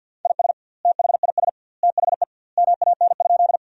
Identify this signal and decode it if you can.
Morse code, heard as a keyed tone, about 50 wpm, 700 Hz. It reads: IS THIS THE MAN?
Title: 2x Word-Spacing - Sentences from Top 200 Words - 50wpm